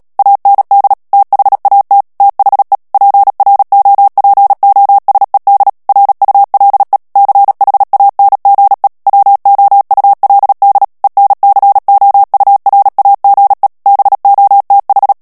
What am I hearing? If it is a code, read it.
Morse code, heard as a keyed tone, about 37 wpm, 800 Hz. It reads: AND THAT THE PROPOSED RULE CHANGE WOULD ENCOURAGE BOTH
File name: morse mp3 / ANGLAIS texte clair 3 vitesse 35 mots minute